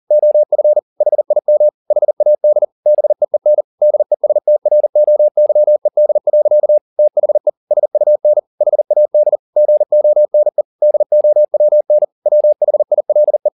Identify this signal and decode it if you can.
Morse code, heard as a keyed tone, about 40 wpm, 600 Hz. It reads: OF HIM HAD BEEN DESTROYED. THE SUN HAD GONE DOWN WHILE
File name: the old man and the sea 2 40wpm